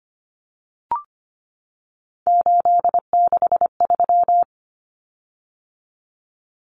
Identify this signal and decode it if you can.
Morse code, heard as a keyed tone, about 25 wpm, 700 Hz. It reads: 863
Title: Common Cut Numbers - 3-Digit 25wpm